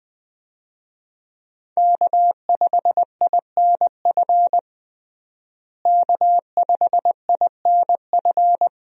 Morse code, heard as a keyed tone, about 20 wpm, 700 Hz. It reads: K5INF K5INF
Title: USA Call Signs - 5 Characters - Repeated 3x - 20wpm